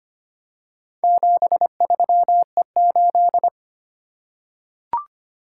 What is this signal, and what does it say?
Morse code, heard as a keyed tone, about 25 wpm, 700 Hz. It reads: 73E8
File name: practice / 25wpm / Four Letter-Number Combinations - 25wpm